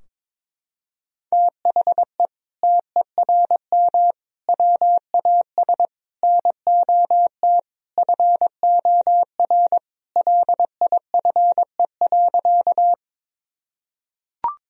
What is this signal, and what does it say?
Morse code, heard as a keyed tone, about 22 wpm, 700 Hz. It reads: THE TERM WAS NOT FOR LIFE.
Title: Sentences from Top 1000 Words - Review Part 4 - 22wpm